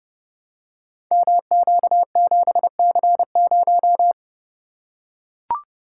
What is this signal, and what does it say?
Morse code, heard as a keyed tone, about 30 wpm, 700 Hz. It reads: MQ7C0
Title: Code Groups - 5 Letter:Numbers - 30wpm